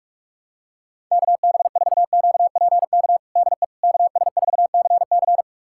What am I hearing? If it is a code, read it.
Morse code, heard as a keyed tone, about 46 wpm, 700 Hz. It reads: KB4QPK DE KS4CC
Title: USA Callsign DE Callsign 45wpm